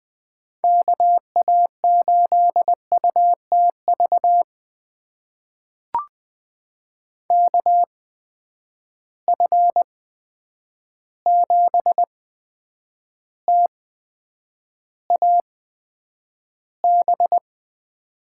Morse code, heard as a keyed tone, about 20 wpm, spaced as for 9 wpm, 700 Hz. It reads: KA8UTV KF7TAB
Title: USA Call Signs - 6 Characters - 8x Character Spacing - 20wpm